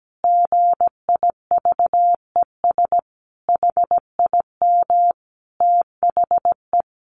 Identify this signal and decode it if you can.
Morse code, heard as a keyed tone, about 17 wpm, 700 Hz. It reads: GIVES HIM THE
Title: Easy Phrases 17wpm